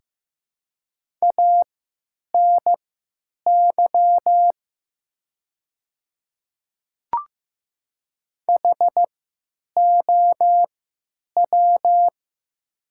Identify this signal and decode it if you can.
Morse code, heard as a keyed tone, about 15 wpm, spaced as for 9 wpm, 700 Hz. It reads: ANY HOW
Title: 3x Character Spacing - Top 200 Words - 15wpm